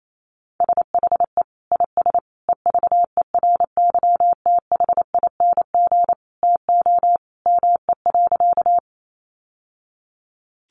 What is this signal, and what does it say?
Morse code, heard as a keyed tone, about 28 wpm, 700 Hz. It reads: SHE IS EVERYTHING TO ME.
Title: Sentences from Top 700 Words 28wpm